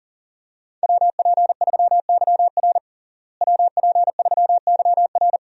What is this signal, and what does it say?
Morse code, heard as a keyed tone, about 40 wpm, 700 Hz. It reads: WP3YR WP3YR
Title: USA Call Signs - 5 Characters - Repeated 2x - 40wpm